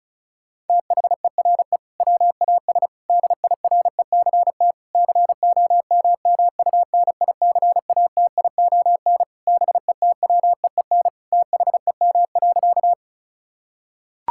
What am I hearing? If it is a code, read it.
Morse code, heard as a keyed tone, about 36 wpm, 700 Hz. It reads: THERE WAS DIRECT COMMUNICATION BETWEEN THEM.
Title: Sentences from Top 1000 Words - Review Part 6 - 35wpm